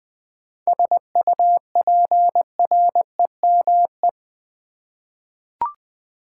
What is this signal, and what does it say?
Morse code, heard as a keyed tone, about 20 wpm, 690 Hz. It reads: SUPREME
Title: 7 Letter Words - First 3 Letters Given - 20wpm